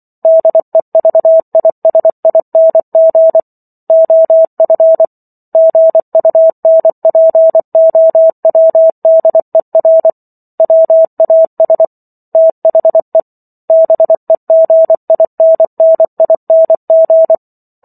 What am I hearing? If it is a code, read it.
Morse code, heard as a keyed tone, about 24 wpm, 650 Hz. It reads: DEVISING OF GUNPOWDER WAS THE BEGINNING